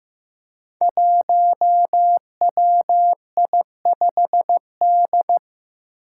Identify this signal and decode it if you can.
Morse code, heard as a keyed tone, about 15 wpm, 700 Hz. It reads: 1WI5D
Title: Five Challenging Letter-Number Combinations 15wpm